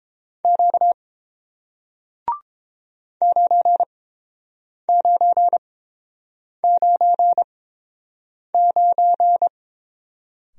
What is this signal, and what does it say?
Morse code, heard as a keyed tone, about 30 wpm, spaced as for 7 wpm, 700 Hz. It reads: Q 9999